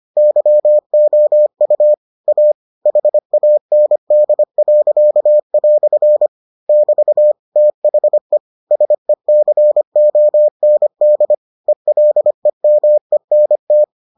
Morse code, heard as a keyed tone, about 25 wpm, 600 Hz. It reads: YOU A HAND." <BT> THE SECOND ELEMENT